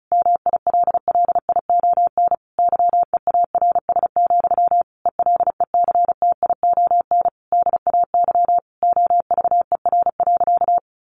Morse code, heard as a keyed tone, about 34 wpm, 700 Hz. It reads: MILLION YEARS, ELECTION DAY OVER.
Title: A Sound Of Thunder 35wpm